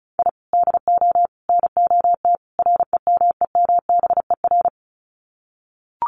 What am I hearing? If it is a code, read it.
Morse code, heard as a keyed tone, about 34 wpm, 700 Hz. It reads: I DO NOT REMEMBER
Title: Sets of 4 Words 35wpm